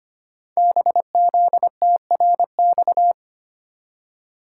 Morse code, heard as a keyed tone, about 25 wpm, 700 Hz. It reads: BZTRX
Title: Code Groups - 5 Letter:Numbers - 25wpm